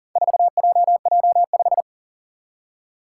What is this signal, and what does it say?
Morse code, heard as a keyed tone, about 40 wpm, 700 Hz. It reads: 4JJ5
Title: Four Challenging Letter-Number Combinations 40wpm